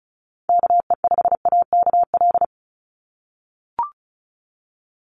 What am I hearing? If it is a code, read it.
Morse code, heard as a keyed tone, about 34 wpm, 700 Hz. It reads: KE5AKL
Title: NA SOTA Chaser Callsigns 35wpm